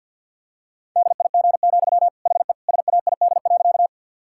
Morse code, heard as a keyed tone, about 50 wpm, 700 Hz. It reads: DIG, HE SAID.
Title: Sentences from Top 1600 Words 50wpm